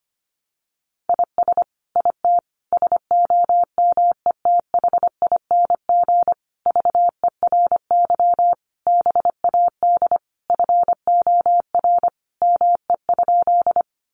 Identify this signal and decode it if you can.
Morse code, heard as a keyed tone, about 25 wpm, 700 Hz. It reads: IS IT SOMETHING VERY BAD FOR ME?
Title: Sentences from Top 700 Words 25wpm